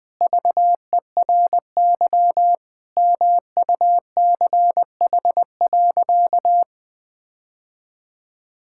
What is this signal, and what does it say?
Morse code, heard as a keyed tone, about 20 wpm, 700 Hz. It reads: VERY MUCH.
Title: Sentences with 3-Word Phrases - 20wpm